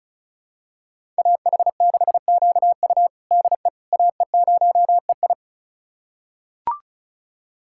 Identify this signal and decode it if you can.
Morse code, heard as a keyed tone, about 34 wpm, 700 Hz. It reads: AH6QU DE AE0EI
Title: USA Callsign DE Callsign - 5 Characters - 35wpm